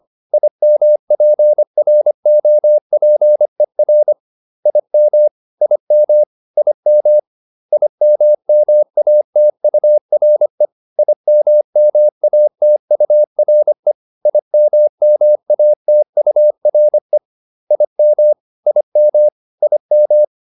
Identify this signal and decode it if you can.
Morse code, heard as a keyed tone, about 25 wpm, 600 Hz. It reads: IMPROPER IM IM IM IMMATURE IMMATURE IMMATURE IM IM IM